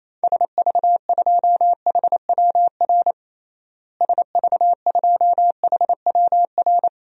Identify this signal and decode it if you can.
Morse code, heard as a keyed tone, about 28 wpm, 700 Hz. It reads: SV2HWR SV2HWR
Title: DXCC-100 Call Signs - 6 Characters - Repeated 2x - 28wpm